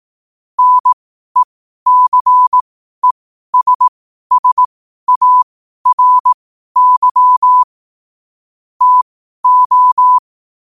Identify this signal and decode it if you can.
Morse code, heard as a keyed tone, about 18 wpm, spaced as for 12 wpm, 1000 Hz. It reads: NECESSARY TO
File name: qotdfiles / 12wpm